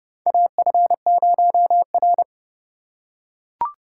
Morse code, heard as a keyed tone, about 30 wpm, 700 Hz. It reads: AF0R
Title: USA Call Signs - 4 Characters - 3x Character Spacing - 30wpm